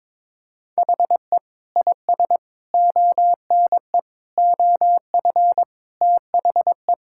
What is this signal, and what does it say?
Morse code, heard as a keyed tone, about 22 wpm, 700 Hz. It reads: HE IS ONE OF THE
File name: practice / 22wpm / Sentences with 8-Word Phrases - 22wpm